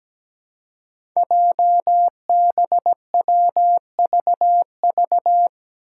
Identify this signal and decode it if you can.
Morse code, heard as a keyed tone, about 17 wpm, 700 Hz. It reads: JBWVV
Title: Five Challenging Letter-Number Combinations 17wpm